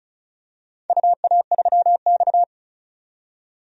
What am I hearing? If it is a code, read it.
Morse code, heard as a keyed tone, about 36 wpm, 700 Hz. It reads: UA3X
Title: DXCC-100 Call Signs - 4 Characters - 35wpm